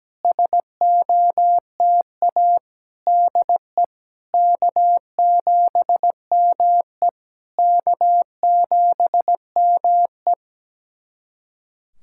Morse code, heard as a keyed tone, about 17 wpm, 700 Hz. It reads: SOTA DE K7ME K7ME
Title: SOTA QSO Practice 17wpm